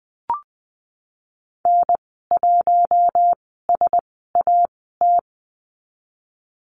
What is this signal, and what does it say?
Morse code, heard as a keyed tone, about 20 wpm, spaced as for 15 wpm, 700 Hz. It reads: N1SAT
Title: 055 Lesson 10g - Letter r - Call Signs 20wpm